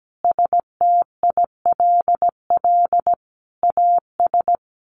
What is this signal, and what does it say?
Morse code, heard as a keyed tone, about 17 wpm, 700 Hz. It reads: STILL AS
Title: Sentences from Top 800 Words 17wpm